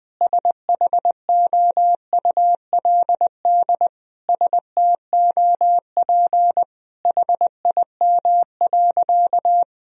Morse code, heard as a keyed tone, about 20 wpm, 700 Hz. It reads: SHOULD STOP HIM.
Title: Sentences from Top 1000 Words - Review Part 1 - 20wpm